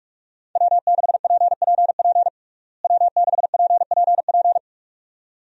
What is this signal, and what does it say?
Morse code, heard as a keyed tone, about 46 wpm, 700 Hz. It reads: W6PPP W6PPP
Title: USA Call Signs - 5 Characters - Repeated 2x - 45wpm